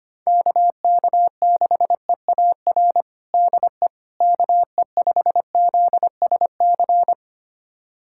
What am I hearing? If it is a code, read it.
Morse code, heard as a keyed tone, about 25 wpm, 700 Hz. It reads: KK6EAR DE KE5ZSC
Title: USA Callsign DE Callsign 25wpm